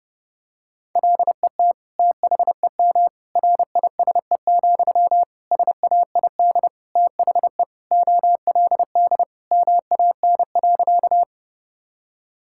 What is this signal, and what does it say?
Morse code, heard as a keyed tone, about 32 wpm, 700 Hz. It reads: LET THEM RISE, SAID THE OLD MAN.
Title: Sentences from Top 1000 Words - Review Part 6 - 30wpm